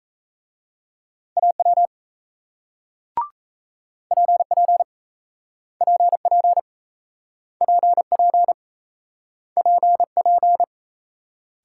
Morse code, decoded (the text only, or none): AW PP PP PP PP